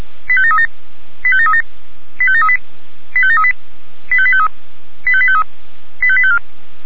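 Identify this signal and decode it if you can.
Five-tone selective call (ZVEI), heard as quick runs of five tones, tones of about 80 ms. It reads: 85737 85737 85738 85738 85853 85853 85854
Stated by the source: ZVEI five-tone selective call 85722 received by SDR.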